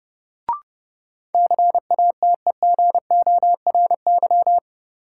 Morse code, heard as a keyed tone, about 31 wpm, 700 Hz. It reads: CATEGORY
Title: Sentences from Top 1000 Words - Review Part 7 - New Word Drill - 30wpm